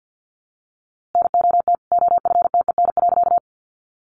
Morse code, heard as a keyed tone, about 50 wpm, 700 Hz. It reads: NOT OFTEN.
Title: A Sound Of Thunder 50wpm